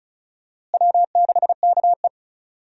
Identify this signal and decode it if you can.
Morse code, heard as a keyed tone, about 34 wpm, 700 Hz. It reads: W6KE
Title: USA Call Signs - 4 Characters - Repeated 3x - 35wpm